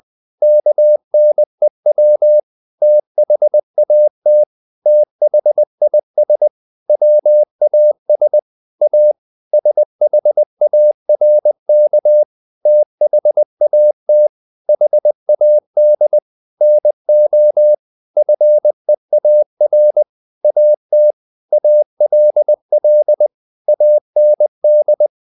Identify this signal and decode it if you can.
Morse code, heard as a keyed tone, about 20 wpm, 600 Hz. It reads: KNEW THAT THIS WAS A SHARK THAT HAD NO FEAR AT ALL AND
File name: the old man and the sea part 2 20wpm